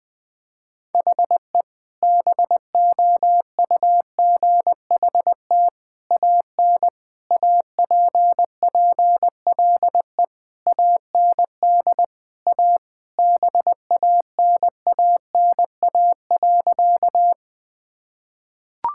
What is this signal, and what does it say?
Morse code, heard as a keyed tone, about 20 wpm, 700 Hz. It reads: HE BOUGHT AN APPLE AND A BANANA.
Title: Sentences with 2-Word Phrases - Spoken Beginning - 20wpm